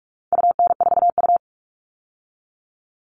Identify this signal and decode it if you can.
Morse code, heard as a keyed tone, about 44 wpm, 700 Hz. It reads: UN4U
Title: All Cut Numbers - 4-Digit 45wpm